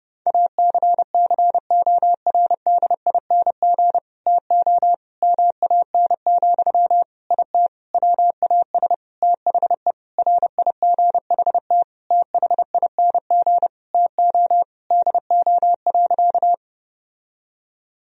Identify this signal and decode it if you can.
Morse code, heard as a keyed tone, about 30 wpm, 700 Hz. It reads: ACCORDING TO MAN, IT WAS THE RIGHT THING TO DO.
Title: Sentences from Top 1000 Words - Review Part 4 - 30wpm